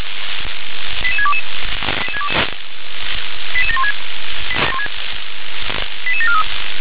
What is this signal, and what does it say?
Five-tone selective call (ZVEI), heard as quick runs of five tones, tones of about 70 ms. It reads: 88622 88716 88743